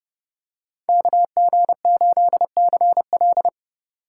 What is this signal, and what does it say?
Morse code, heard as a keyed tone, about 29 wpm, 700 Hz. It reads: KG8CL